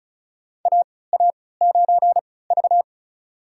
Morse code, heard as a keyed tone, about 36 wpm, spaced as for 20 wpm, 700 Hz. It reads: AA9V
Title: USA Call Signs - 4 Characters - 3x Character Spacing - 35wpm